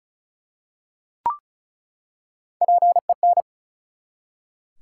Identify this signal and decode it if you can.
Morse code, heard as a keyed tone, about 36 wpm, 700 Hz. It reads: PEN
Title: Three Letter Words - 35wpm